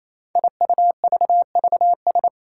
Morse code, heard as a keyed tone, about 27 wpm, 700 Hz. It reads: IUVVS
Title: Five Challenging Letter-Number Combinations 28wpm